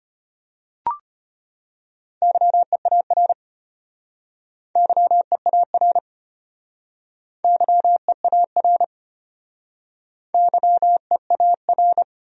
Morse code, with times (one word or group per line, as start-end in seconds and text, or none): 2.2-3.3 YEAR
4.8-6.0 YEAR
7.4-8.8 YEAR
10.3-12.0 YEAR